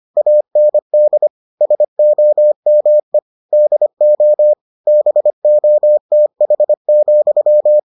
Morse code, heard as a keyed tone, about 25 wpm, 600 Hz. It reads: AND SOME DO BOTH,